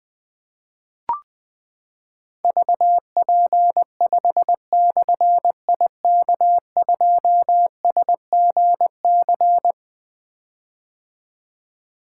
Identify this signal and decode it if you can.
Morse code, heard as a keyed tone, about 20 wpm, 700 Hz. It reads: VP5/IK2SGC
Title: Difficult Call Signs 20wpm